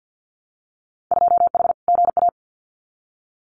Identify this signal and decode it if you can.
Morse code, heard as a keyed tone, about 50 wpm, 700 Hz. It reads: 2H GA
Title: Winter Field Day Exchanges 50wpm